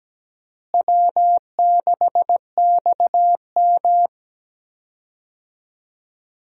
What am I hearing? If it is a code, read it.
Morse code, heard as a keyed tone, about 17 wpm, 700 Hz. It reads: W6XM
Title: USA Call Signs - 4 Characters - Repeated 2x - 17wpm